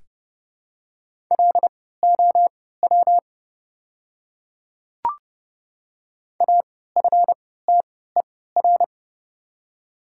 Morse code, heard as a keyed tone, about 30 wpm, spaced as for 17 wpm, 700 Hz. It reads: LOW AFTER
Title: 3x Character Spacing - Top 200 Words - 30wpm